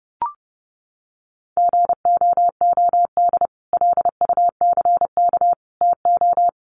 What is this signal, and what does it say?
Morse code, heard as a keyed tone, about 30 wpm, 700 Hz. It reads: GOOD LUCK TO
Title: Sentences from Top 1500 Words 30wpm